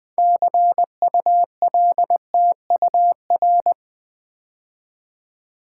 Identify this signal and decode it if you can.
Morse code, heard as a keyed tone, about 20 wpm, 700 Hz. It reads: CULTUR
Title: Build A Word - 3 Letters At A Time - 8-Letter Words - 20wpm